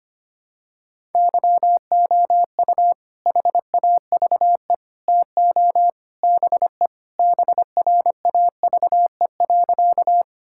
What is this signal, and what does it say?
Morse code, heard as a keyed tone, about 25 wpm, 700 Hz. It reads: YOU HAVE TO BE BRAVE.
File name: practice / 25wpm / Sentences with 3-Word Phrases - 25wpm